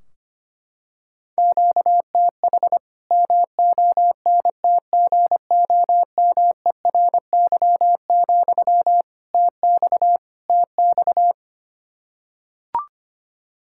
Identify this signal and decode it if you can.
Morse code, heard as a keyed tone, about 25 wpm, 700 Hz. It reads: QTH MONTGOMERY, TX TX
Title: Practice QSO - Exchange 1 - Beginning - 25wpm